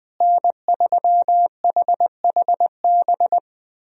Morse code, heard as a keyed tone, about 20 wpm, 700 Hz. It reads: N3HHB
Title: USA Call Signs - 5 Characters - Repeated 2x - 20wpm